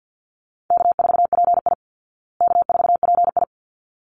Morse code, heard as a keyed tone, about 50 wpm, 700 Hz. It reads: K4PI K4PI
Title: Easy Call Signs Repeated 2x - 50wpm